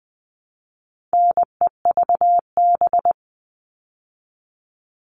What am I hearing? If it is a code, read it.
Morse code, heard as a keyed tone, about 20 wpm, 700 Hz. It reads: NEVB
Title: All Cut Numbers - 4-Digit 20wpm